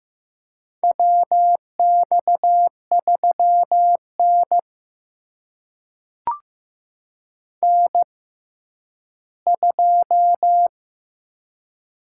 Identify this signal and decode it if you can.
Morse code, heard as a keyed tone, about 15 wpm, spaced as for 10 wpm, 700 Hz. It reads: WX3N N2
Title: USA Call Signs - 4 Characters - 6x Character Spacing - 15wpm